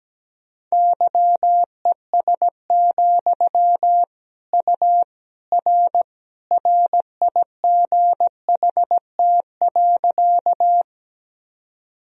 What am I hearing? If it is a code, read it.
Morse code, heard as a keyed tone, about 17 wpm, 700 Hz. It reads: YES, U R RIGHT.
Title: Sentences from Top 500 Words and Abbreviations 17wpm